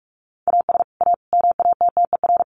Morse code, heard as a keyed tone, about 44 wpm, 700 Hz. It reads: AS A MATTER
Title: Sets of 5 Words 45wpm